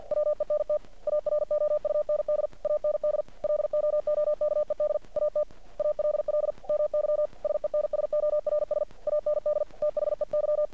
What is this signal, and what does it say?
Morse code, heard as a keyed tone, about 50 wpm, 600 Hz. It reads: WENT AROUND AND LOOKED AT ALL MY SENSORS AND THEY
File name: archived / CRSnet